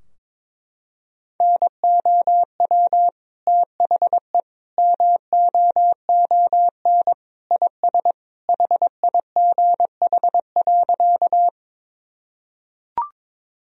Morse code, heard as a keyed tone, about 22 wpm, 700 Hz. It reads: NOW THE MOON IS HIGH.
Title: Sentences from Top 1000 Words - Review Part 3 - 22wpm